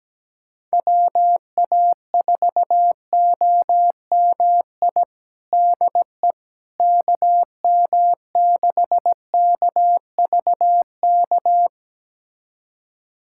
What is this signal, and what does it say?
Morse code, heard as a keyed tone, about 17 wpm, 700 Hz. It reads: WA4OMI DE KM6KVK
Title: USA Callsign DE Callsign - 6 Characters - 17wpm